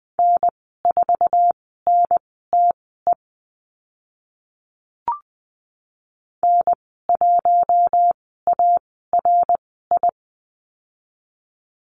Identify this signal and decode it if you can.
Morse code, heard as a keyed tone, about 20 wpm, spaced as for 14 wpm, 700 Hz. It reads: N4NTE N1ARI
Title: 062 Lesson 11g - Letter h - Call Signs 20wpm